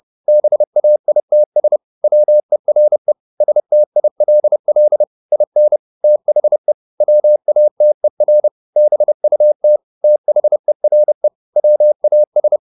Morse code, heard as a keyed tone, about 30 wpm, 600 Hz. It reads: BAITS WERE STILL IN THE WATER BUT THERE WAS
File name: the old man and the sea / the old man and the sea 1 30wpm